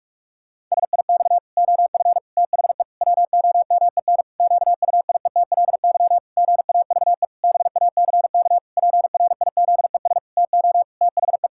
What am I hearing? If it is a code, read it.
Morse code, heard as a keyed tone, about 46 wpm, 700 Hz. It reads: SIX OF THE WOMEN QUIETLY GAVE BACK PRIZES TO THE
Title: Pangram Challenge v2 - 45wpm